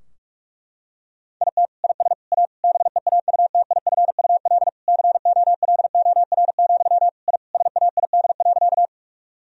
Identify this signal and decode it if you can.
Morse code, heard as a keyed tone, about 41 wpm, 700 Hz. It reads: IT IS A BEAUTIFUL COLOR, I SAID.
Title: Sentences from Top 1000 Words - Review Part 2 - 45wpm